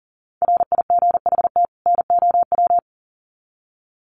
Morse code, heard as a keyed tone, about 40 wpm, 700 Hz. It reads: RIGHT NOW
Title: Sets of 2 Words 40wpm